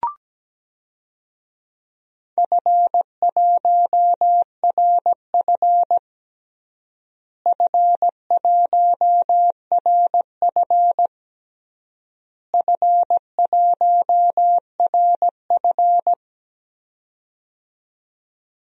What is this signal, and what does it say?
Morse code, heard as a keyed tone, about 17 wpm, 700 Hz. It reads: F1RF F1RF F1RF
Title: DXCC-100 Call Signs - 4 Characters - Repeated 3x - 17wpm